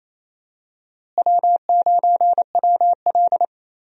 Morse code, heard as a keyed tone, about 28 wpm, 700 Hz. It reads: W9WL
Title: USA Call Signs - 4 Characters - Repeated 3x - 28wpm